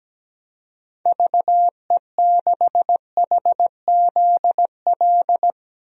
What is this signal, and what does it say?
Morse code, heard as a keyed tone, about 17 wpm, 700 Hz. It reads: VE6HZL
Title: DXCC-100 Call Signs - 6 Characters - 17wpm